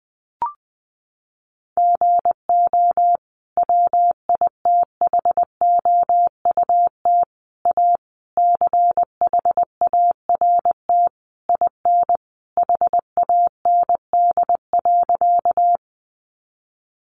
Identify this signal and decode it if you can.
Morse code, heard as a keyed tone, about 20 wpm, 700 Hz. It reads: GO WITHOUT A CHART IN HAND.
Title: Sentences from Top 1300 Words 20wpm